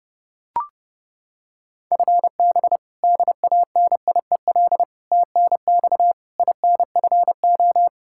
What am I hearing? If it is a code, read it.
Morse code, heard as a keyed tone, about 32 wpm, 700 Hz. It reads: FB DANIEL TNX INFO
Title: Practice QSO - Exchange 3 - More Info - 30wpm